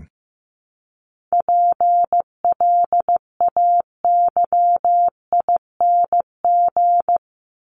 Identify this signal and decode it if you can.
Morse code, heard as a keyed tone, about 15 wpm, 700 Hz. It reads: PLAYING
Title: ING Suffix - Top 100 Words with Base Words 15wpm